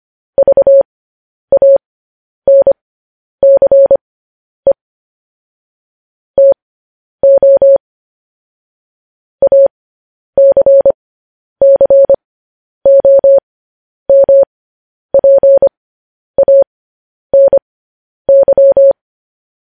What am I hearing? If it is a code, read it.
Morse code, heard as a keyed tone, about 25 wpm, spaced as for 10 wpm, 550 Hz. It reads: VANCE TO ACCOMPANY
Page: news-headlines-in-morse-code-at-10-wpm